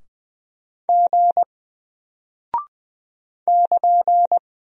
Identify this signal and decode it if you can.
Morse code, heard as a keyed tone, about 20 wpm, 700 Hz. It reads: G <KN>
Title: Single Character (Letter:Number:Punctuation:Prosign) Alt 20wpm